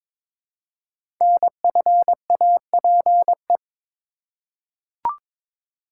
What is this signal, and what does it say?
Morse code, heard as a keyed tone, about 22 wpm, 700 Hz. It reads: NFAPE
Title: Code Groups - 5 Letters - 22wpm